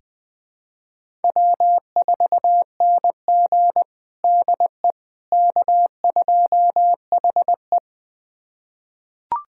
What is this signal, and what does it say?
Morse code, heard as a keyed tone, about 20 wpm, 700 Hz. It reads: W4NG DE K2HE
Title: USA Callsign DE Callsign - 4 Characters - 20wpm